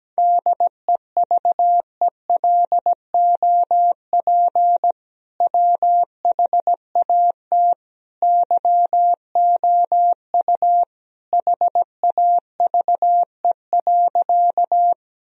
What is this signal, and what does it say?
Morse code, heard as a keyed tone, about 17 wpm, 700 Hz. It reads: DEVELOP WHAT YOU HAVE.
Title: Sentences from Top 1000 Words - Review Part 2 - 17wpm